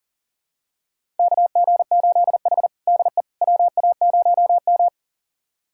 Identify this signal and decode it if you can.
Morse code, heard as a keyed tone, about 37 wpm, 700 Hz. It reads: KG8H DE WA0M
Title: USA Callsign DE Callsign - 4 Characters - 40wpm